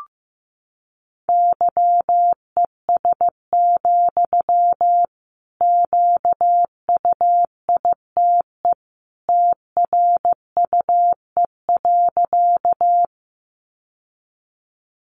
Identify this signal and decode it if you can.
Morse code, heard as a keyed tone, about 15 wpm, 700 Hz. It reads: YES, QUITE TRUE.
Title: Sentences from Top 700 Words 15wpm